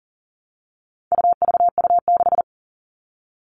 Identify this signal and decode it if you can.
Morse code, heard as a keyed tone, about 40 wpm, 700 Hz. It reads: UVU6
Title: All Cut Numbers - 4-Digit 40wpm